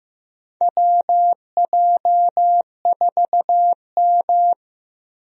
Morse code, heard as a keyed tone, about 15 wpm, 700 Hz. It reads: WJ4M
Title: USA Call Signs - 4 Characters - Repeated 3x - 15wpm